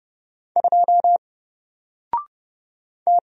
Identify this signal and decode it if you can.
Morse code, heard as a keyed tone, about 30 wpm, 700 Hz. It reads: 2 T
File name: Single-Letter-Number-30wpm